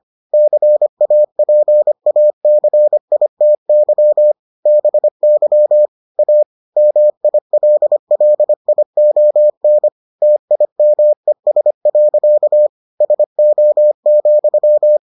Morse code, heard as a keyed tone, about 25 wpm, 600 Hz. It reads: CAPACITY BY A MILLION TIMES. SO,